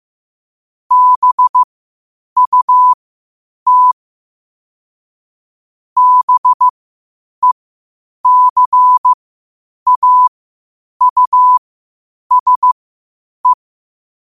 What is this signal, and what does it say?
Morse code, heard as a keyed tone, about 15 wpm, spaced as for 8 wpm, 1000 Hz. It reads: BUT BECAUSE
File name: qotdfiles / 8wpm